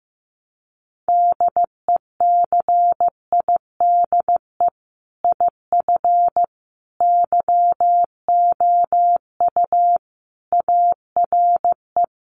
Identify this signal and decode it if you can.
Morse code, heard as a keyed tone, about 15 wpm, 700 Hz. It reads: DECIDE IF YOU ARE
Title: Sentences from Top 500 Words 15wpm